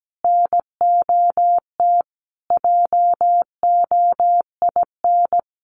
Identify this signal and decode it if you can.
Morse code, heard as a keyed tone, about 17 wpm, 700 Hz. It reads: NOT JOIN
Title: Sentences from Top 1400 Words 17wpm